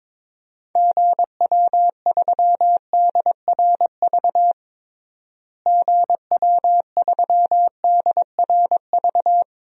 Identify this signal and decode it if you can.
Morse code, heard as a keyed tone, about 22 wpm, 700 Hz. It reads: GW3DRV GW3DRV
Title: DXCC-100 Call Signs - 6 Characters - Repeated 3x - 22wpm